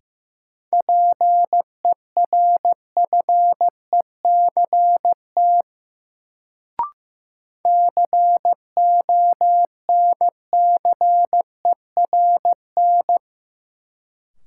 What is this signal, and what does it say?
Morse code, heard as a keyed tone, about 15 wpm, 700 Hz. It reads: PERFECT CONCERN
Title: Sentences from Top 1000 Words - New Word Drill - 15wpm